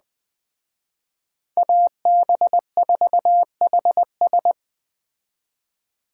Morse code, heard as a keyed tone, about 20 wpm, 700 Hz. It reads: AB4HS